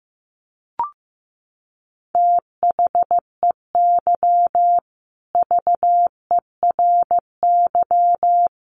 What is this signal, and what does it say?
Morse code, heard as a keyed tone, about 15 wpm, 700 Hz. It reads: THEY VERY
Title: Sentences from Top 1400 Words 15wpm